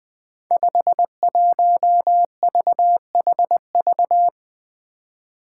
Morse code, heard as a keyed tone, about 20 wpm, 700 Hz. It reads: 51VHV
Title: Five Challenging Letter-Number Combinations 20wpm